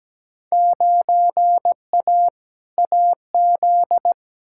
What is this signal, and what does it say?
Morse code, heard as a keyed tone, about 17 wpm, 700 Hz. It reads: 9A AZ
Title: Field Day Exchanges - 17wpm